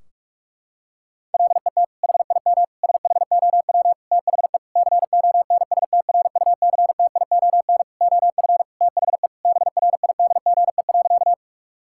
Morse code, heard as a keyed tone, about 46 wpm, 700 Hz. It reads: LET HIM SHOW THE CONSTRUCTION OF THE BRIDGE.
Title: Sentences from Top 1600 Words 45wpm